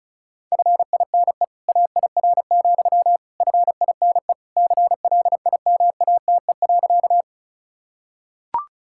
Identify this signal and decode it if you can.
Morse code, heard as a keyed tone, about 36 wpm, 700 Hz. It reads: FINE AIR, FINE CLIMATE.
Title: Sentences from Top 1600 Words 35wpm